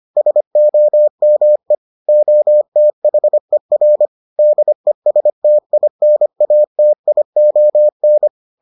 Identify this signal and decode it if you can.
Morse code, heard as a keyed tone, about 25 wpm, 600 Hz. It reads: SOME OTHER DESTINATION